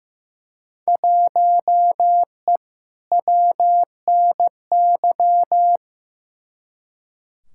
Morse code, heard as a keyed tone, about 15 wpm, 700 Hz. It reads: 1E WNY
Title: Field Day Exchanges - 15wpm